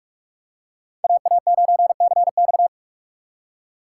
Morse code, heard as a keyed tone, about 41 wpm, 700 Hz. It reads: AA9CX